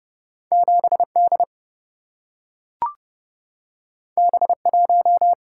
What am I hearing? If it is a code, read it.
Morse code, heard as a keyed tone, about 30 wpm, 700 Hz. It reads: 7 D B 1
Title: Single-Letter Single-Number 30wpm